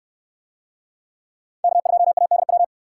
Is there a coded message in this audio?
Yes: Morse code from 1.6 to 2.6 s, heard as a keyed tone, about 60 wpm, 700 Hz. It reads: K2ADW